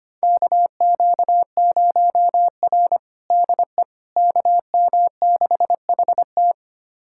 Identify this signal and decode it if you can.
Morse code, heard as a keyed tone, about 25 wpm, 700 Hz. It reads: KQ0R DE KM6HT
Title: USA Callsign DE Callsign 25wpm